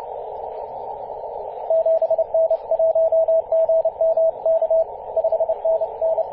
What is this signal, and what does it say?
Morse code, heard as a keyed tone, about 30 wpm, 650 Hz, with radio noise. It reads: 7N1GMK 5NN
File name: 5w0xg-18-075mhz-cw